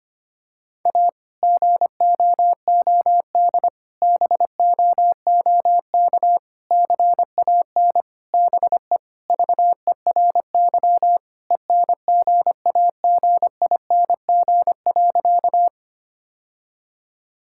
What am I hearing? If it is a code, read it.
Morse code, heard as a keyed tone, about 25 wpm, 700 Hz. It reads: A GOOD BOOK CAN BE VERY ENGAGING.
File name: practice / 25wpm / Sentences with 2-Word Phrases - 25wpm